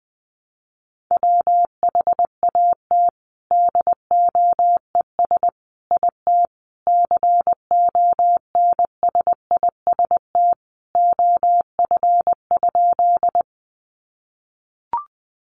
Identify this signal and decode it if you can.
Morse code, heard as a keyed tone, about 19 wpm, 700 Hz. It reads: WHAT DOES IT CONSIST OF?
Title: Sentences from Top 1400 Words 20wpm